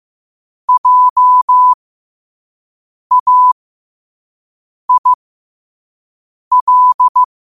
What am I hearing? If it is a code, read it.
Morse code, heard as a keyed tone, about 15 wpm, spaced as for 5 wpm, 1000 Hz. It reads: JAIL
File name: qotdfiles / 5wpm